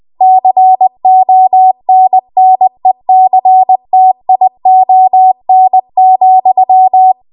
Morse code, heard as a keyed tone, about 20 wpm, 750 Hz. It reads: CONNECTION,
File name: Morse / 20WPM